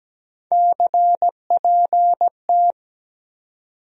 Morse code, heard as a keyed tone, about 17 wpm, 700 Hz. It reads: CPT
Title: USA Call Sign Postfixes - 3 Letters 17wpm